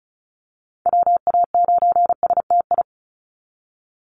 Morse code, heard as a keyed tone, about 34 wpm, 700 Hz. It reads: WA9STI
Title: NA SOTA Chaser Callsigns 35wpm